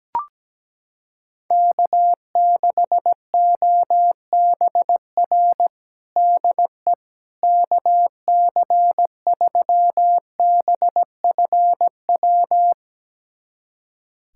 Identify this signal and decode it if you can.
Morse code, heard as a keyed tone, about 17 wpm, 700 Hz. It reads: K6OBR DE KC3BFW
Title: USA Callsign DE Callsign 17wpm